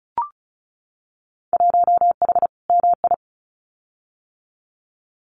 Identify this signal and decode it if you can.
Morse code, heard as a keyed tone, about 36 wpm, 700 Hz. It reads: 1H MI
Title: Winter Field Day Exchanges 35wpm